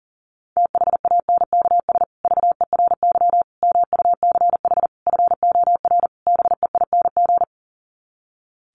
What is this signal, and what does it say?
Morse code, heard as a keyed tone, about 40 wpm, 700 Hz. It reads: THANKS VERY MUCH FOR BEING
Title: Sets of 5 Words Encore 40wpm